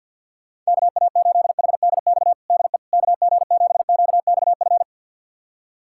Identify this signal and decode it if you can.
Morse code, heard as a keyed tone, about 51 wpm, 700 Hz. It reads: KA8HDX DE KG7QXF
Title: USA Callsign DE Callsign - 6 Characters - 50wpm